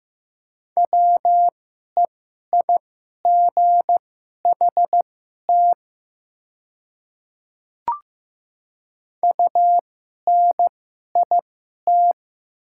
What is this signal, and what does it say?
Morse code, heard as a keyed tone, about 15 wpm, spaced as for 11 wpm, 700 Hz. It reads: WEIGHT UNIT
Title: 2x Character Spacing - Top 500 Words - 15wpm